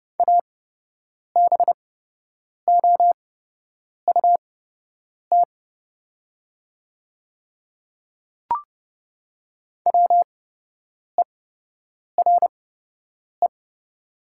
Morse code, heard as a keyed tone, about 30 wpm, spaced as for 8 wpm, 700 Hz. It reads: ABOUT WERE
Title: 8x Character Spacing - Top 100 Words - 30wpm